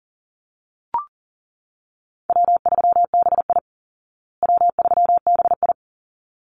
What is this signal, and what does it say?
Morse code, heard as a keyed tone, about 37 wpm, 700 Hz. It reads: W3BI W3BI
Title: Easy Call Signs Repeated 2x - 40wpm